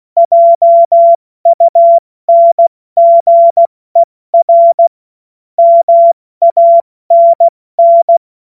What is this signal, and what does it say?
Morse code, heard as a keyed tone, about 16 wpm, 680 Hz. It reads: JUNGER MANN
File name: Morsekurs / Witze / Witz 05 März 2024 70 F80